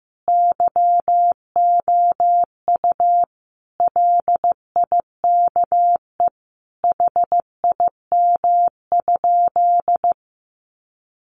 Morse code, heard as a keyed tone, about 15 wpm, 700 Hz. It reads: YOU LIKE HIM?
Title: Sentences from Top 100 Words 15wpm